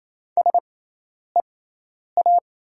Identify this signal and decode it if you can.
Morse code, heard as a keyed tone, about 27 wpm, spaced as for 10 wpm, 700 Hz. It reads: SEA
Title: 6x Character Spacing - Top 300 Words - 28wpm